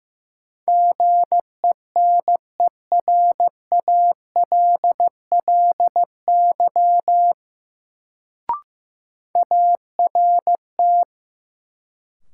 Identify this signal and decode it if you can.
Morse code, heard as a keyed tone, about 15 wpm, 700 Hz. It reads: GENERALLY ART
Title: Sentences from Top 600 Words - New Word Drill - 15wpm